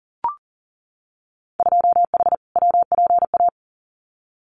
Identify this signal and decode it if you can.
Morse code, heard as a keyed tone, about 40 wpm, 700 Hz. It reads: 2H WPA
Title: Winter Field Day Exchanges 40wpm